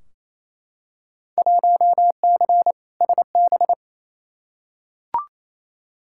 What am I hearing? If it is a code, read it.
Morse code, heard as a keyed tone, about 28 wpm, 700 Hz. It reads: 1C SB
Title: Field Day Exchanges Encore - 28wpm